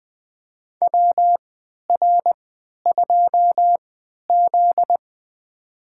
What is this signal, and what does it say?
Morse code, heard as a keyed tone, about 20 wpm, spaced as for 11 wpm, 700 Hz. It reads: WR2Z
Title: USA Call Signs - 4 Characters - 3x Character Spacing - 20wpm